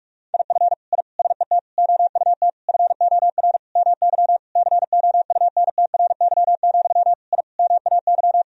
Morse code, heard as a keyed tone, about 45 wpm, 700 Hz. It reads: IF I SET OUT FOR MY COUNTRY, I MAY
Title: Sentences from Top 1000 Words - Review Part 2 - 45wpm